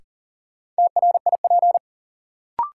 Morse code, heard as a keyed tone, about 40 wpm, 700 Hz. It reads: TRIP
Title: Sentences from Top 1000 Words - Review Part 4 - New Word Drill - 40wpm